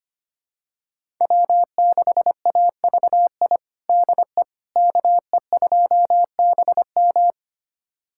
Morse code, heard as a keyed tone, about 25 wpm, 700 Hz. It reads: W6AVI DE KE2BM
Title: USA Callsign DE Callsign - 5 Characters - 25wpm